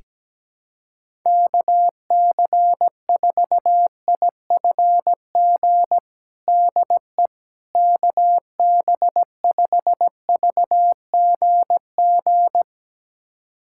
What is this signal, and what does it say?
Morse code, heard as a keyed tone, about 17 wpm, 700 Hz. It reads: KC4IFG DE KB5VGG
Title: USA Callsign DE Callsign - 6 Characters - 17wpm